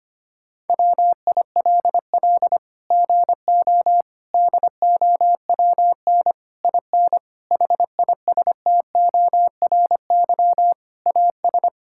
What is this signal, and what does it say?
Morse code, heard as a keyed tone, about 25 wpm, 700 Hz. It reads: WILL GO DOWN IN HISTORY AS
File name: Sets of 9 Words - 25wpm